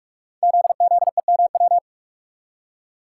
Morse code, heard as a keyed tone, about 44 wpm, 700 Hz. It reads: ZZEMW
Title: Code Groups - 5 Letters - 45wpm